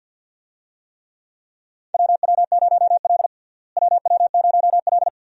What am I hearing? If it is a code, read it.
Morse code, heard as a keyed tone, about 47 wpm, 700 Hz. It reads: WW0L WW0L
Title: USA Call Signs - 4 Characters - Repeated 2x - 50wpm